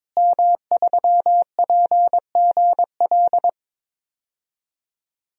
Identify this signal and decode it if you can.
Morse code, heard as a keyed tone, about 22 wpm, 700 Hz. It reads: M3PGL
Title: DXCC-100 Call Signs - 5 Characters - Repeated 2x - 22wpm